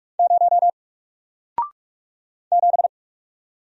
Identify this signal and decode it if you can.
Morse code, heard as a keyed tone, about 45 wpm, 700 Hz. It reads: 0 7
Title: Single Letter-Number Alt 45wpm